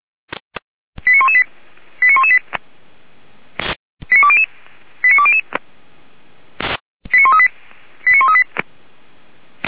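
Five-tone selective call (ZVEI), heard as quick runs of five tones, tones of about 70 ms. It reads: 89118 89118 89200 89200 89138 89138
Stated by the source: ZVEI five-tone selective call received on an SDR.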